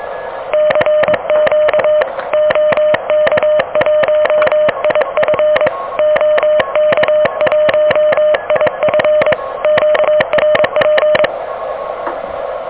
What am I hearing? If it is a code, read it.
Morse code, heard as a keyed tone, about 22 wpm, 610 Hz, with noise. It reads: CQ OK1IF OK1IF QRP
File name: OK QRP 2023 / OK QRP 2023